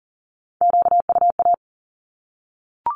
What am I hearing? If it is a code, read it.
Morse code, heard as a keyed tone, about 40 wpm, 700 Hz. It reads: QUA
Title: Q-Codes 40wpm